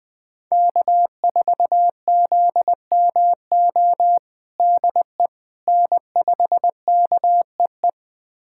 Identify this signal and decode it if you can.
Morse code, heard as a keyed tone, about 21 wpm, 700 Hz. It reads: K4ZMO DE N5KEE
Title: USA Callsign DE Callsign - 5 Characters - 20wpm